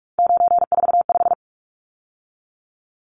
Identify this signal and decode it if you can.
Morse code, heard as a keyed tone, about 44 wpm, 700 Hz. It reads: 945
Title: Numbers - 3-Digit 45wpm